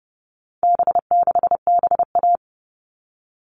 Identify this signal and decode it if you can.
Morse code, heard as a keyed tone, about 30 wpm, 700 Hz. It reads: B6BA
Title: All Cut Numbers - 4-Digit 30wpm